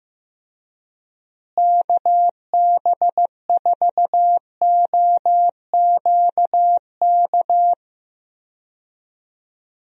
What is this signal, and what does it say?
Morse code, heard as a keyed tone, about 16 wpm, 700 Hz. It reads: KB4OQK
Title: USA Call Signs - 6 Characters - Repeated 3x - 15wpm